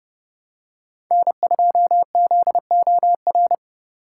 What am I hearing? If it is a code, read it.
Morse code, heard as a keyed tone, about 28 wpm, 700 Hz. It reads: N2ZOR